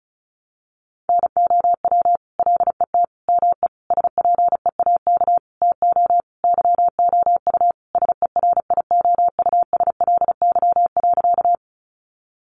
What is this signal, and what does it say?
Morse code, heard as a keyed tone, about 34 wpm, 700 Hz. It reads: NOW LET ME SPEAK TO YOU SERIOUSLY.
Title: Sentences from Top 1200 Words 35wpm